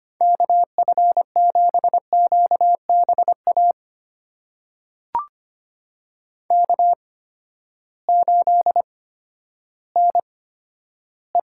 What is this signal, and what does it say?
Morse code, heard as a keyed tone, about 25 wpm, spaced as for 12 wpm, 700 Hz. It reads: KF7QBA K8NE
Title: USA Call Signs - 8x Character Spacing - 25wpm